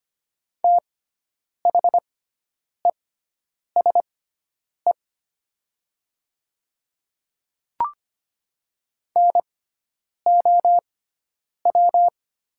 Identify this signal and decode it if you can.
Morse code, heard as a keyed tone, about 25 wpm, spaced as for 9 wpm, 700 Hz. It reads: THESE NOW
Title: 6x Character Spacing - Top 100 Words - 25wpm